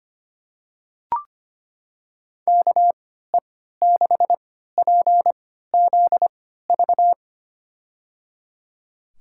Morse code, heard as a keyed tone, about 25 wpm, spaced as for 14 wpm, 700 Hz. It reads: KE6PZV
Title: USA Call Signs - 3x Character Spacing - 25wpm